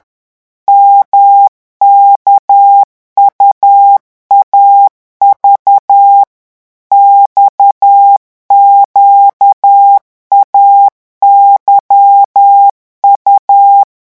Morse code, heard as a keyed tone, about 10 wpm, 800 Hz. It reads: MKUAV XQAYU
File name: morse mp3 / revision2-vitesse 10 mots